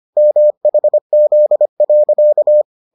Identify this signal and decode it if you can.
Morse code, heard as a keyed tone, about 25 wpm, 600 Hz. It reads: MHZ.